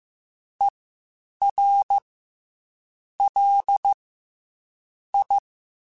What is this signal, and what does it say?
Morse code, heard as a keyed tone, about 15 wpm, 800 Hz. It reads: E R L I